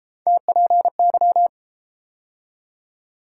Morse code, heard as a keyed tone, about 31 wpm, 700 Hz. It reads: TPY